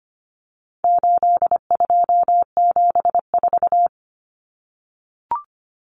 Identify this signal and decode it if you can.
Morse code, heard as a keyed tone, about 25 wpm, 700 Hz. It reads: 8274
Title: Numbers - 4-Digit 25wpm